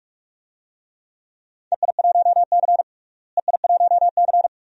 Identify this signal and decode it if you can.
Morse code, heard as a keyed tone, about 46 wpm, 700 Hz. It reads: EI1C EI1C
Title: DXCC-100 Call Signs - 4 Characters - Repeated 2x - 45wpm